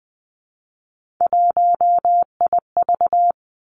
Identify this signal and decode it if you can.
Morse code, heard as a keyed tone, about 20 wpm, 700 Hz. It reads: 1IV
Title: Three Challenging Letter-Number Combinations 20wpm